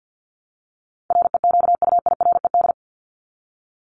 Morse code, heard as a keyed tone, about 50 wpm, 700 Hz. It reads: REQUIRED
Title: ED Suffix - Top 100 Words with Base Words 50wpm